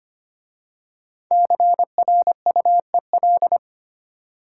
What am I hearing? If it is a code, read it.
Morse code, heard as a keyed tone, about 25 wpm, 700 Hz. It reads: CRUEL
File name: 5 Letter Words - First 3 Letters Given - 25wpm